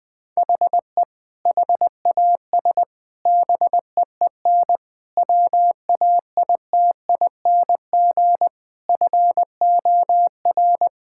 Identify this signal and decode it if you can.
Morse code, heard as a keyed tone, about 20 wpm, 700 Hz. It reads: HE HAS BEEN WAITING FOR
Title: Sentences with 3-Word Phrases - 20wpm